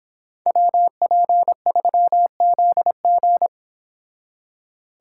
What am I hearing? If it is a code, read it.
Morse code, heard as a keyed tone, about 26 wpm, 700 Hz. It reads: WP3ZG
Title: Speed Racing - USA Call Signs 22wpm